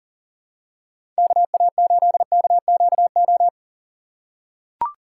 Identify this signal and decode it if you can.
Morse code, heard as a keyed tone, about 37 wpm, 700 Hz. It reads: KA8KQO
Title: USA Call Signs - 2x Character Spacing - 40wpm